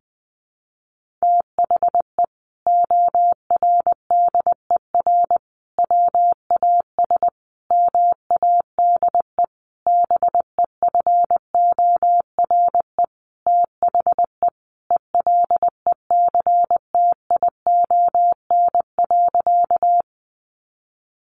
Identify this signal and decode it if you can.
Morse code, heard as a keyed tone, about 20 wpm, 700 Hz. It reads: THE ORDER WAS MADE BEFORE THE ELECTION.
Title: Sentences from Top 1500 Words 20wpm